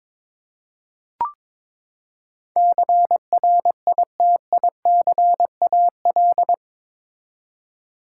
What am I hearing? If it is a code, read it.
Morse code, heard as a keyed tone, about 21 wpm, 700 Hz. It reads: CRITICAL
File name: practice / 22wpm / Eight Letter Words - 22wpm